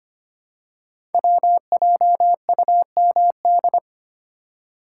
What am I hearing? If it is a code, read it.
Morse code, heard as a keyed tone, about 25 wpm, 700 Hz. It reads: WJUMD
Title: Code Groups - 5 Letters - 25wpm